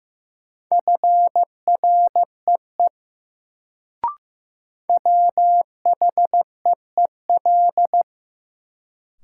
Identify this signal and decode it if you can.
Morse code, heard as a keyed tone, about 15 wpm, 700 Hz. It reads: FREE WHEEL
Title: Sentences from Top 500 Words - New Word Drill - 15wpm